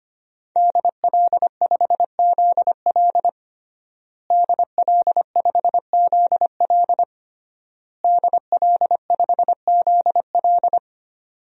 Morse code, heard as a keyed tone, about 25 wpm, 700 Hz. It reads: DL5ZL DL5ZL DL5ZL
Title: DXCC-100 Call Signs - 5 Characters - Repeated 3x - 25wpm